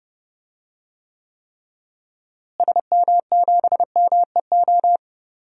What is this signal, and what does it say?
Morse code, heard as a keyed tone, about 30 wpm, 700 Hz. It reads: SM7MEO
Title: DXCC-100 Call Signs - 6 Characters - Repeated 3x - 30wpm